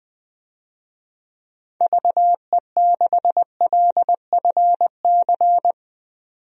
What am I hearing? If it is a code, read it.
Morse code, heard as a keyed tone, about 20 wpm, 700 Hz. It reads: VE6LFC
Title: DXCC-100 Call Signs - 6 Characters - 20wpm